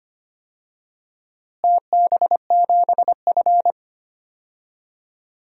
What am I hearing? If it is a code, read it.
Morse code, heard as a keyed tone, about 25 wpm, 700 Hz. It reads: TB7F
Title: DXCC-100 Call Signs - 4 Characters - Repeated 2x - 25wpm